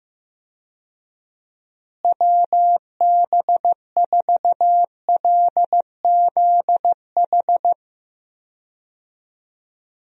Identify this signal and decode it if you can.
Morse code, heard as a keyed tone, about 15 wpm, 700 Hz. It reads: WB4LZH
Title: USA Call Signs - 6 Characters - Repeated 3x - 15wpm